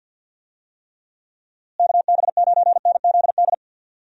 Morse code, heard as a keyed tone, about 46 wpm, 700 Hz. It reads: KB9NZD
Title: USA Call Signs - 6 Characters - Repeated 2x - 50wpm